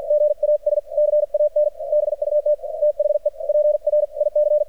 Morse code, heard as a keyed tone, about 53 wpm, spaced as for 46 wpm, 600 Hz. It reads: GAS CAN BUT THE PRIC
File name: archived / CRSnet